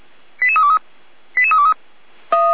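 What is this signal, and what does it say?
Five-tone selective call (ZVEI), heard as quick runs of five tones, tones of about 80 ms. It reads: 88323 88323
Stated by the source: ZVEI five-tone selective call received by SDR, from the Feuerwehr Orlamünde fire brigade audio server.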